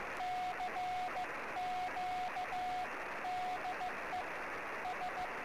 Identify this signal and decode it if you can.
Morse code, heard as a keyed tone, about 15 wpm, 730 Hz. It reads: CQ DE S
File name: sm7lcb / 08 sm5le-p jo98ax 23cm